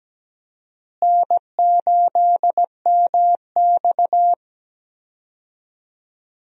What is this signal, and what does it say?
Morse code, heard as a keyed tone, about 17 wpm, 700 Hz. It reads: N8MX
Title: USA Call Signs - 4 Characters - Repeated 2x - 17wpm